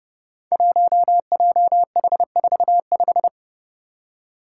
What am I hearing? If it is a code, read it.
Morse code, heard as a keyed tone, about 31 wpm, 700 Hz. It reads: 1JH45
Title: Five Challenging Letter-Number Combinations 30wpm